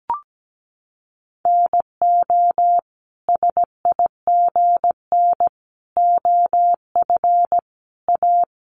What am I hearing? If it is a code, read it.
Morse code, heard as a keyed tone, about 17 wpm, 700 Hz. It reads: NO SIGN OF A
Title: Sentences from Top 1000 Words 17wpm